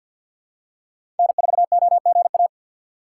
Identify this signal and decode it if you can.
Morse code, heard as a keyed tone, about 53 wpm, 700 Hz. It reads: N4OGA
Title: USA Call Signs - 5 Characters - 6x Character Spacing - 50wpm